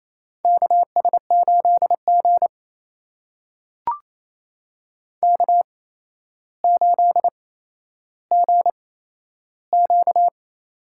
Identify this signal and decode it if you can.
Morse code, heard as a keyed tone, about 28 wpm, spaced as for 12 wpm, 700 Hz. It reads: KS8G K8GQ
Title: USA Call Signs - 4 Characters - 8x Character Spacing - 28wpm